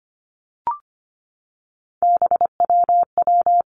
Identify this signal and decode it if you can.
Morse code, heard as a keyed tone, about 25 wpm, 700 Hz. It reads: BWW
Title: Three Challenging Letter-Number Combinations 25wpm